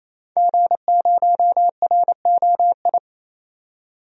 Morse code, heard as a keyed tone, about 28 wpm, 700 Hz. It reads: G0ROI
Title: DXCC-100 Call Signs - 5 Characters - 28wpm